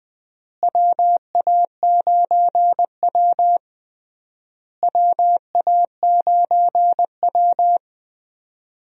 Morse code, heard as a keyed tone, about 20 wpm, 700 Hz. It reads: WA9W WA9W
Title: USA Call Signs - 4 Characters - Repeated 3x - 20wpm